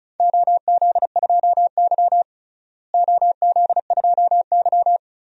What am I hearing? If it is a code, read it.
Morse code, heard as a keyed tone, about 36 wpm, 700 Hz. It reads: OZ2Y OZ2Y
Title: DXCC-100 Call Signs - 4 Characters - Repeated 2x - 35wpm